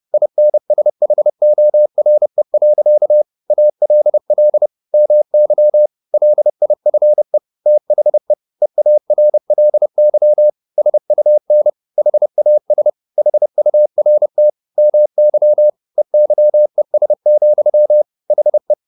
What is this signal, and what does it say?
Morse code, heard as a keyed tone, about 30 wpm, 600 Hz. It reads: INSHORE. ALL MY LIFE THE EARLY SUN HAS HURT MY EYES, HE
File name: the old man and the sea / the old man and the sea 1 30wpm